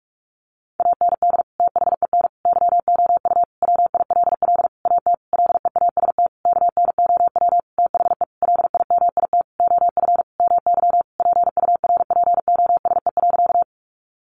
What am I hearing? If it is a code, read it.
Morse code, heard as a keyed tone, about 44 wpm, 700 Hz. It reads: AND THEN YOU WILL AT LEAST KNOW THE LIMIT OF MY PURPOSE.
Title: Sentences from Top 1500 Words 45wpm